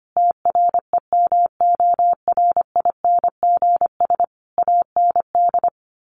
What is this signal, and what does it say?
Morse code, heard as a keyed tone, about 25 wpm, 700 Hz. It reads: TREMORINGS AND
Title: A Sound Of Thunder 25wpm